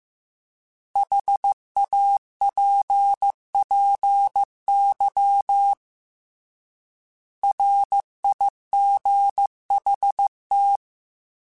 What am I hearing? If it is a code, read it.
Morse code, heard as a keyed tone, about 15 wpm, 800 Hz. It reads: HAPPY RIGHT
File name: Short Runs / Short Runs-mp3s / Words @13